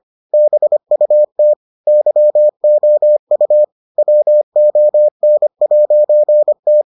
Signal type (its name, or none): Morse code